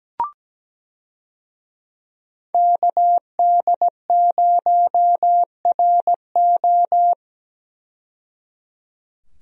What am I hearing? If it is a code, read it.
Morse code, heard as a keyed tone, about 17 wpm, 700 Hz. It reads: KD0RO
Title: USA Call Signs - 5 Characters - 17wpm